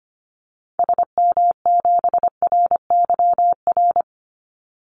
Morse code, heard as a keyed tone, about 25 wpm, 700 Hz. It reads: SM7RYR
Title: Call Signs Repeated 2x - 25wpm